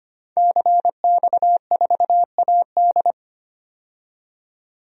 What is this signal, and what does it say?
Morse code, heard as a keyed tone, about 25 wpm, 700 Hz. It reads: CX4AD
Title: DXCC-100 Call Signs - 5 Characters - 25wpm